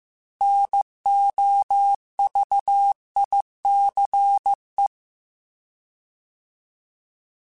Morse code, heard as a keyed tone, about 15 wpm, 800 Hz. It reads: NOVICE